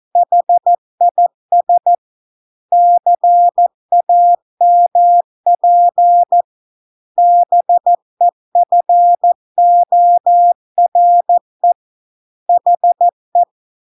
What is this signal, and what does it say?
Morse code, heard as a keyed tone, about 14 wpm, 700 Hz. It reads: HIS CAMP BEFORE HE